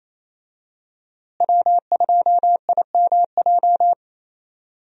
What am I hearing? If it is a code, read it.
Morse code, heard as a keyed tone, about 28 wpm, 700 Hz. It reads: W2IMJ
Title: USA Call Signs - 5 Characters - Repeated 3x - 28wpm